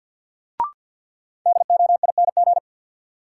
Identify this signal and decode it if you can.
Morse code, heard as a keyed tone, about 51 wpm, 700 Hz. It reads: DOING
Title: Sentences from Top 1400 Words - New Word Drill Alt - 50wpm